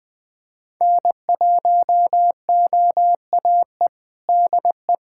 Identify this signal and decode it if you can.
Morse code, heard as a keyed tone, about 20 wpm, 700 Hz. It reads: N1OAE DE
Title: USA Callsign DE Callsign - 5 Characters - 20wpm